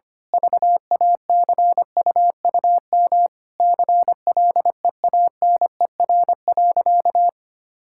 Morse code, heard as a keyed tone, about 25 wpm, 700 Hz. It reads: VACUUM CLEANER.
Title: Pangram Challenge v2 - 25wpm